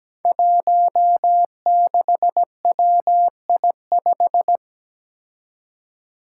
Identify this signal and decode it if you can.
Morse code, heard as a keyed tone, about 17 wpm, 700 Hz. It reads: 16WI5
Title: Five Challenging Letter-Number Combinations 17wpm